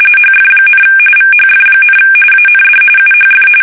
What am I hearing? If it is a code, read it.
RTTY, about 45.45 baud. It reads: HF, VHF, AND UHF
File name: fsk850 / battleshipnj